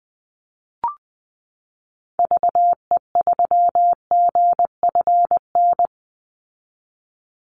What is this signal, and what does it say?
Morse code, heard as a keyed tone, about 20 wpm, 700 Hz. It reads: VE3GFN
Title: Call Signs Repeated 2x - 20wpm